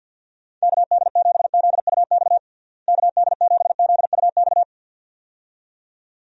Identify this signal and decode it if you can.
Morse code, heard as a keyed tone, about 49 wpm, 700 Hz. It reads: KD7ZUX KD7ZUX
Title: USA Call Signs - 6 Characters - Repeated 2x - 50wpm